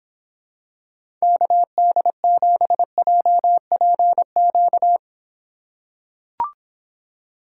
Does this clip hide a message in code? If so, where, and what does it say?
Yes: Morse code from 1.2 to 5.0 s, heard as a keyed tone, about 26 wpm, 700 Hz. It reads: KD7JPQ